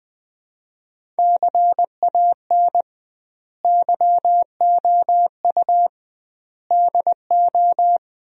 Morse code, heard as a keyed tone, about 20 wpm, 700 Hz. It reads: CAN YOU DO
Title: 2x Word-Spacing - Sentences from Top 100 Words - 20wpm